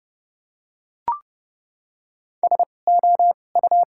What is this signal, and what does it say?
Morse code, heard as a keyed tone, about 31 wpm, spaced as for 22 wpm, 700 Hz. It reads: SOU
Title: 2x Character Spacing - Top 500 Words - 30wpm